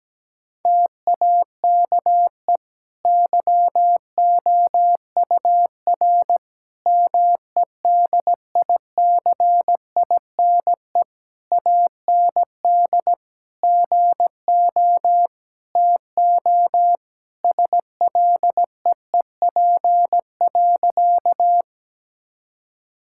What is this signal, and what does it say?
Morse code, heard as a keyed tone, about 17 wpm, 700 Hz. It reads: TAKE YOUR MEDICINE AND GO TO SLEEP.
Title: Sentences from Top 1000 Words - Review Part 6 - 17wpm